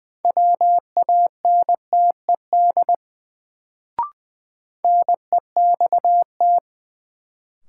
Morse code, heard as a keyed tone, about 20 wpm, 700 Hz. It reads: WANTED NEXT
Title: Sets of 2 Words - New Word Drill - 20wpm